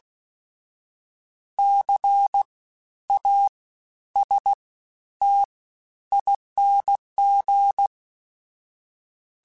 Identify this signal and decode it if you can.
Morse code, heard as a keyed tone, about 16 wpm, 800 Hz. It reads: C A S T ING
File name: Special INGs-mp3s / Special INGs-mp3s / SlowToFaster ing @15